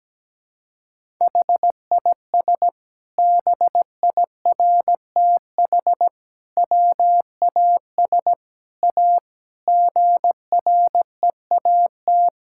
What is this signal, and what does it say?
Morse code, heard as a keyed tone, about 17 wpm, 700 Hz. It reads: HIS BIRTH WAS A GREAT
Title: Sentences from Top 1000 Words - Review Part 5 - 17wpm